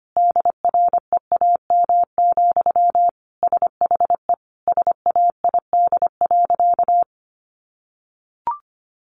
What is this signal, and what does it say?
Morse code, heard as a keyed tone, about 25 wpm, 700 Hz. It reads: DREAM, SHE SAID.
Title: Sentences from Top 1500 Words 25wpm